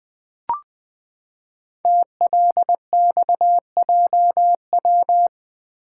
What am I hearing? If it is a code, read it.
Morse code, heard as a keyed tone, about 20 wpm, 700 Hz. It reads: TLXJW
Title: Code Groups - 5 Letters - 20wpm